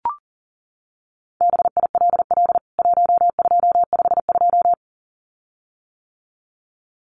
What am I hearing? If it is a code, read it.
Morse code, heard as a keyed tone, about 40 wpm, 700 Hz. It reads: BILL 1252
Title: CWT Exchange 40wpm